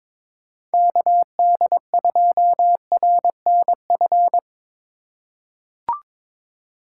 Morse code, heard as a keyed tone, about 21 wpm, 700 Hz. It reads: KD2RNF
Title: USA Call Signs - 6 Characters - 3x Character Spacing - 22wpm